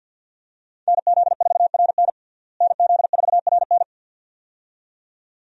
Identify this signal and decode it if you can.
Morse code, heard as a keyed tone, about 47 wpm, 700 Hz. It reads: NZ4RN NZ4RN
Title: USA Call Signs - 5 Characters - Repeated 2x - 50wpm